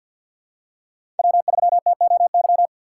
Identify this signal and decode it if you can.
Morse code, heard as a keyed tone, about 52 wpm, 700 Hz. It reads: W3TOY
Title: USA Call Signs - 6x Character Spacing - 50wpm